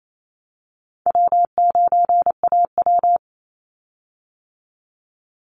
Morse code, heard as a keyed tone, about 28 wpm, 700 Hz. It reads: W9AW
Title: Easy Call Signs - 28wpm